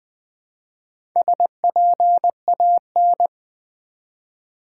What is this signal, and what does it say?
Morse code, heard as a keyed tone, about 20 wpm, 700 Hz. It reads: SPAN
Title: Build A Word - 1 Letter At A Time - 3 to 5-Letter Words - 20wpm